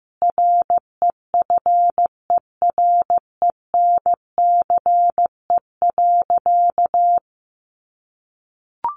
Morse code, heard as a keyed tone, about 15 wpm, 700 Hz. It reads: REFERENCE.
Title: Sentences from Top 1200 Words 15wpm